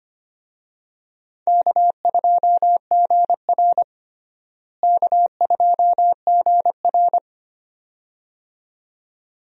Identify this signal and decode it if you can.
Morse code, heard as a keyed tone, about 25 wpm, 700 Hz. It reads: K2GR K2GR
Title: USA Call Signs - 4 Characters - Repeated 2x - 25wpm